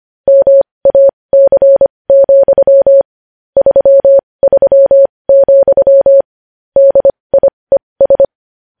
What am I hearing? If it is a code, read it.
Morse code, heard as a keyed tone, about 25 wpm, spaced as for 20 wpm, 550 Hz. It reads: MAC, 33, DIES
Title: News Headlines at 20